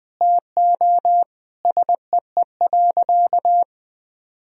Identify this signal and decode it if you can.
Morse code, heard as a keyed tone, about 21 wpm, 700 Hz. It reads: TO SEE.
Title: Sentences from Top 1000 Words - Review Part 4 - 20wpm